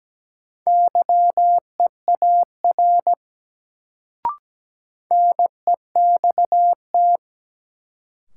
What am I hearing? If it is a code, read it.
Morse code, heard as a keyed tone, about 17 wpm, 700 Hz. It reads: YEAR NEXT
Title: Sets of 2 Words - New Word Drill - 17wpm